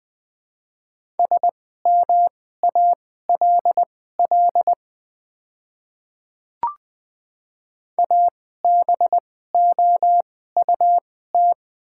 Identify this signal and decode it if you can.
Morse code, heard as a keyed tone, about 20 wpm, spaced as for 14 wpm, 700 Hz. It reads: SMALL ABOUT
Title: 2x Character Spacing - Top 300 Words - 20wpm